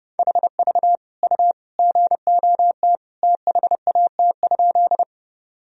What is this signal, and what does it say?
Morse code, heard as a keyed tone, about 30 wpm, 700 Hz. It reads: HV U GOT THAT?
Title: Sentences from Top 300 Words and Abbreviations 30wpm